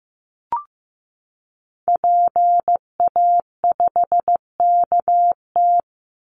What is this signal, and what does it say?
Morse code, heard as a keyed tone, about 15 wpm, 700 Hz. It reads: PA5KT
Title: Call Signs Repeated 3x - 15wpm